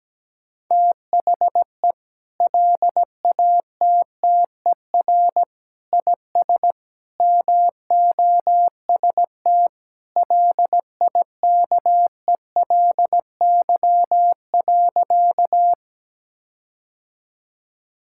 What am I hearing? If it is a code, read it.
Morse code, heard as a keyed tone, about 17 wpm, 700 Hz. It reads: THE LATTER IS MOST LIKELY.
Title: Sentences from Top 1600 Words 17wpm